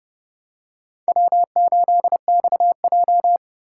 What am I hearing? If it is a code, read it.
Morse code, heard as a keyed tone, about 28 wpm, 700 Hz. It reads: W8XJ